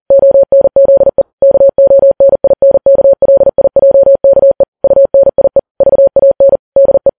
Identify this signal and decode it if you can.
Morse code, heard as a keyed tone, about 40 wpm, 550 Hz. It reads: ONZE KONINKLIJKE UNIE VAN DE
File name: Files Web tls / Morse on9fv / morse40kps